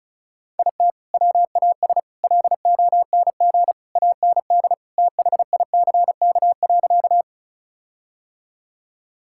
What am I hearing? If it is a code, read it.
Morse code, heard as a keyed tone, about 36 wpm, 700 Hz. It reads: IT WAS LONG AND THICK.
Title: Sentences from Top 1600 Words 35wpm